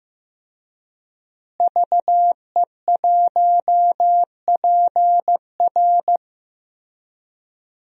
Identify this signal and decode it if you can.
Morse code, heard as a keyed tone, about 15 wpm, 700 Hz. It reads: VE1PR
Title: DXCC-100 Call Signs - 5 Characters - Repeated 3x - 15wpm